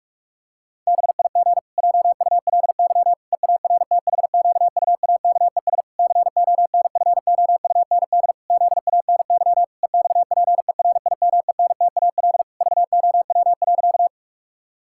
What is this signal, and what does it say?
Morse code, heard as a keyed tone, about 46 wpm, 700 Hz. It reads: BIG JULY EARTHQUAKES CONFOUND ZANY EXPERIMENTAL VOW.
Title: Pangram Challenge v2 - 45wpm